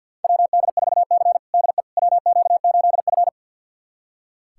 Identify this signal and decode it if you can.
Morse code, heard as a keyed tone, about 47 wpm, 700 Hz. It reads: WD4C DE WQ8F
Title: USA Callsign DE Callsign - 4 Characters - 50wpm